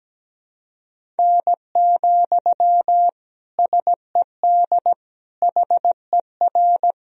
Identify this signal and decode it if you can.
Morse code, heard as a keyed tone, about 17 wpm, 700 Hz. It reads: N, SED HER
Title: Sentences from Top 300 Words and Abbreviations 17wpm